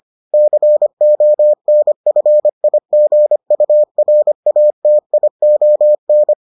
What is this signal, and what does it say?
Morse code, heard as a keyed tone, about 25 wpm, 600 Hz. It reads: CONFIGURATION